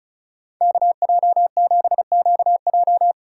Morse code, heard as a keyed tone, about 45 wpm, 700 Hz. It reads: KJ7QJ